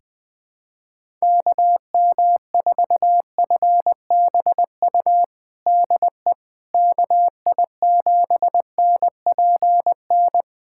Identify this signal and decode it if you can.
Morse code, heard as a keyed tone, about 20 wpm, 700 Hz. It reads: KM4FBU DE KI7NPN
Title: USA Callsign DE Callsign 20wpm